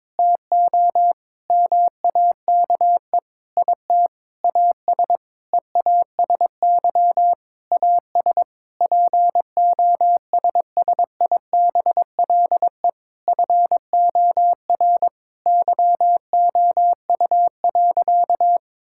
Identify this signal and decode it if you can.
Morse code, heard as a keyed tone, about 22 wpm, 700 Hz. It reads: TO MAKE IT AS EASY AS POSSIBLE FOR YOU.
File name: Sentences with 7-Word Phrases - 22wpm